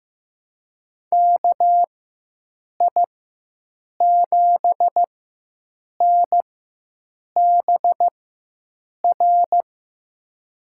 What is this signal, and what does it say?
Morse code, heard as a keyed tone, about 15 wpm, spaced as for 7 wpm, 700 Hz. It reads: KI7NBR
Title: USA Call Signs - 6 Characters - 4x Character Spacing - 15wpm